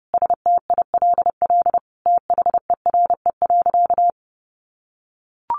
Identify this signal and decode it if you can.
Morse code, heard as a keyed tone, about 30 wpm, 700 Hz. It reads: STILL THERE.
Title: Sentences from Top 1500 Words 30wpm